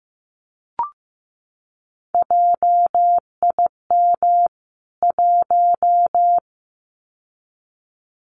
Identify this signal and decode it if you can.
Morse code, heard as a keyed tone, about 15 wpm, 700 Hz. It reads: JIM 1
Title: CWT Exchange 15wpm